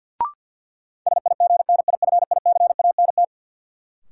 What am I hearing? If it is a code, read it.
Morse code, heard as a keyed tone, about 52 wpm, 700 Hz. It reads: SIGNIFICANT
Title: Sentences from Top 1000 Words - Review Part 6 - New Word Drill - 50wpm